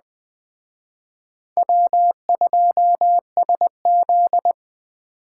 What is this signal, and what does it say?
Morse code, heard as a keyed tone, about 21 wpm, 700 Hz. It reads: W2SZ